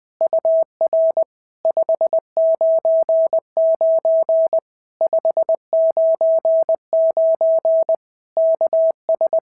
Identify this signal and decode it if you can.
Morse code, heard as a keyed tone, about 20 wpm, 650 Hz. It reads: UR 599 599 KS
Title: POTA QSO Practice - 20wpm